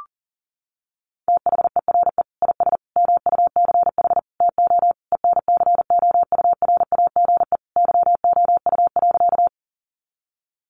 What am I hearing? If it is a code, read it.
Morse code, heard as a keyed tone, about 40 wpm, 700 Hz. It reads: THERE IS MUCH TO ENCOURAGE YOU.
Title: Sentences from Top 1400 Words 40wpm